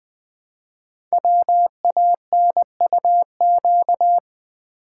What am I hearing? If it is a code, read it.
Morse code, heard as a keyed tone, about 20 wpm, 700 Hz. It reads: WANUQ
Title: Code Groups - 5 Letter:Numbers - 20wpm